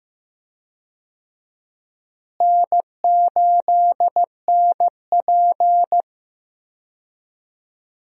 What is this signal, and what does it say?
Morse code, heard as a keyed tone, about 15 wpm, 700 Hz. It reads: N8NP
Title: USA Call Signs - 4 Characters - Repeated 2x - 15wpm